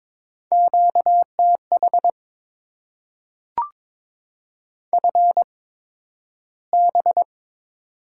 Morse code, heard as a keyed tone, about 22 wpm, spaced as for 12 wpm, 700 Hz. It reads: QTH FB
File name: Top 20 QSO Elements - 8x Character Spacing - 22wpm